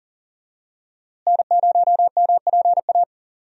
Morse code, heard as a keyed tone, about 40 wpm, 700 Hz. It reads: N0MPA
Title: USA Call Signs - 5 Characters - Repeated 2x - 40wpm